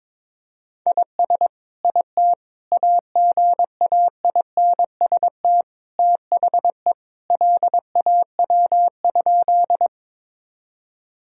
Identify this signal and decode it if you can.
Morse code, heard as a keyed tone, about 22 wpm, 700 Hz. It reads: IS IT AGAINST THE LAW?
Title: Sentences from Top 1000 Words - Review Part 3 - 22wpm